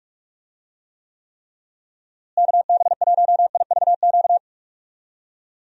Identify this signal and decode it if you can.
Morse code, heard as a keyed tone, about 45 wpm, 700 Hz. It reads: KB1IVQ
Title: USA Call Signs - 6 Characters - Repeated 3x - 45wpm